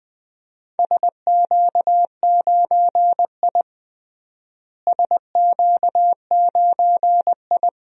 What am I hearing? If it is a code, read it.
Morse code, heard as a keyed tone, about 20 wpm, 700 Hz. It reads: SQ9I SQ9I
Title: DXCC-100 Call Signs - 4 Characters - Repeated 2x - 20wpm